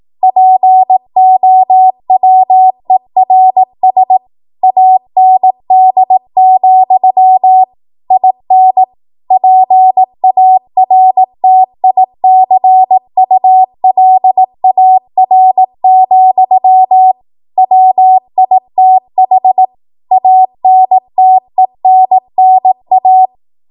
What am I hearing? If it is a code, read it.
Morse code, heard as a keyed tone, about 18 wpm, 750 Hz. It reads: POWERS AND, IN PARTICULAR, WITH ANTENNA
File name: Morse / 18WPM